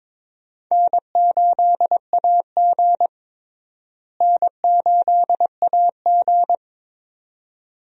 Morse code, heard as a keyed tone, about 21 wpm, 700 Hz. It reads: N8AG N8AG
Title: USA Call Signs - 4 Characters - Repeated 2x - 22wpm